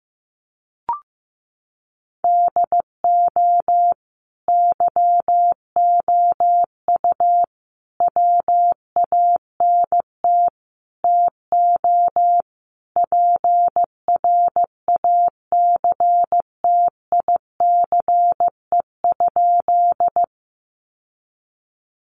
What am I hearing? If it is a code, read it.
Morse code, heard as a keyed tone, about 15 wpm, 700 Hz. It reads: DO YOU WANT TO PRACTICE?
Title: Sentences from Top 700 Words 15wpm